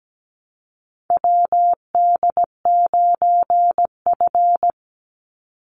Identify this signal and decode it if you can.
Morse code, heard as a keyed tone, about 17 wpm, 700 Hz. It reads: WD9F
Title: NA SOTA Chaser Callsigns 17wpm